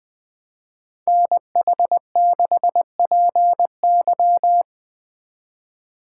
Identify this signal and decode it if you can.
Morse code, heard as a keyed tone, about 21 wpm, 700 Hz. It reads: NH6PY